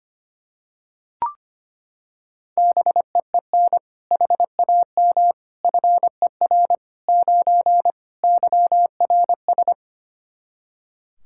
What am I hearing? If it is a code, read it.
Morse code, heard as a keyed tone, about 25 wpm, 700 Hz. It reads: BEEN HAM FER 9 YRS
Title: Practice QSO - Exchange 3 - More Info - 25wpm